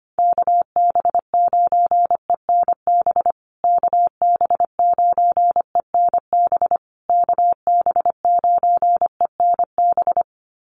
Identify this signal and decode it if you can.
Morse code, heard as a keyed tone, about 25 wpm, 700 Hz. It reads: KB9ENB KB9ENB KB9ENB
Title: USA Call Signs Repeated 3x 25wpm